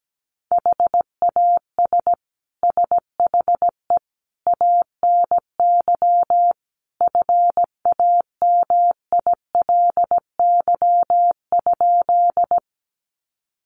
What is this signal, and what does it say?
Morse code, heard as a keyed tone, about 17 wpm, 700 Hz. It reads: HAS SHE ANY FAMILY?
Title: Sentences from Top 400 Words 17wpm